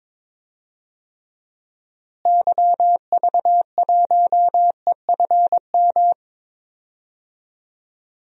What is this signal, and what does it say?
Morse code, heard as a keyed tone, about 21 wpm, 700 Hz. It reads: YV1EFM
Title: DXCC-100 Call Signs - 6 Characters - Repeated 3x - 22wpm